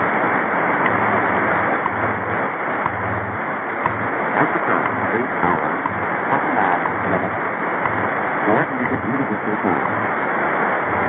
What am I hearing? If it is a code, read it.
Radio time signal, with speech.
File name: WWV 0830